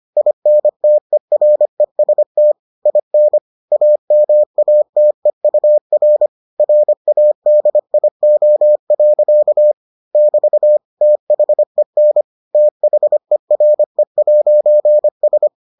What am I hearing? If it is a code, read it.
Morse code, heard as a keyed tone, about 25 wpm, 600 Hz. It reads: INTEREST IN AMATEUR RADIO. <BT> THEN THERE'S